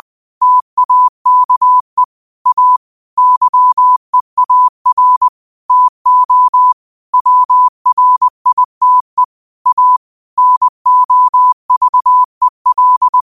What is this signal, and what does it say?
Morse code, heard as a keyed tone, about 20 wpm, 1000 Hz. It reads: TAKE A YEAR TO WRITE A NOVEL